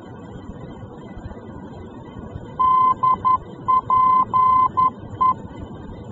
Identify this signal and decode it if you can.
Morse code, heard as a keyed tone, about 11 wpm, 1030 Hz. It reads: DPE
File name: france / mx / DPE-F-VOR